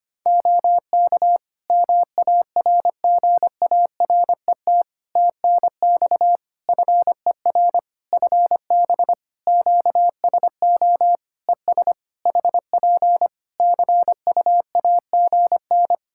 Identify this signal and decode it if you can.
Morse code, heard as a keyed tone, about 25 wpm, 700 Hz. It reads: OK MARGARET TNX FER FB QSO ES HP CUAGN
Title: Practice QSO - Exchange 4 - Ending - 25wpm